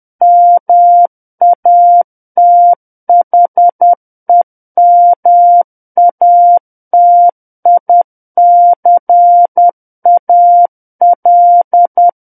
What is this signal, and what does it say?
Morse code, heard as a keyed tone, about 10 wpm, 700 Hz. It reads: MATHEMATICAL
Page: War of the Worlds - 01-Chapter 1 - 10 WPM